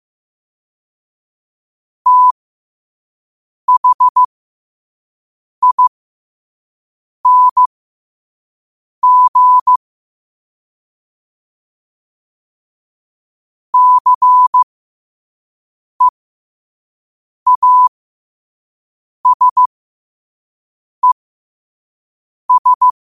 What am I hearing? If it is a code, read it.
Morse code, heard as a keyed tone, about 15 wpm, spaced as for 5 wpm, 1000 Hz. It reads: THING CEASES
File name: qotdfiles / 5wpm